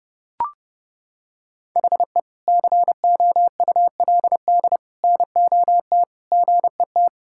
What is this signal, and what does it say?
Morse code, heard as a keyed tone, about 30 wpm, 700 Hz. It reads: HE COULD NOT GET
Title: Sentences from Top 1000 Words - Review Part 5 - 30wpm